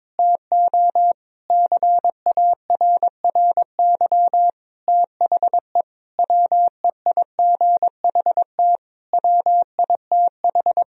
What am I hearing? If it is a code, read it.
Morse code, heard as a keyed tone, about 23 wpm, 700 Hz. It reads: TO CARRY THE WEIGHT WITH
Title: Sentences from Top 1000 Words - Review Part 3 - 22wpm